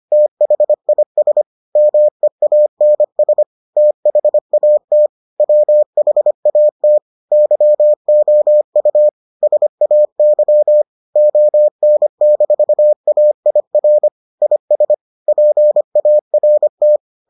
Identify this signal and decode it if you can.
Morse code, heard as a keyed tone, about 25 wpm, 600 Hz. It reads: THIS MEANS THAT WHAT YOU SAY ON-AIR IS PART